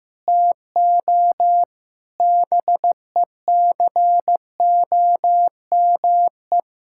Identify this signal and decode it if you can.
Morse code, heard as a keyed tone, about 16 wpm, 700 Hz. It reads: TO BECOME
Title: Sentences from Top 1600 Words 15wpm